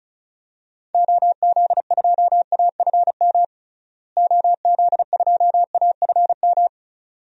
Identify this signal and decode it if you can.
Morse code, heard as a keyed tone, about 36 wpm, 700 Hz. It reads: OZ2AFM OZ2AFM
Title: DXCC-100 Call Signs - 6 Characters - Repeated 2x - 35wpm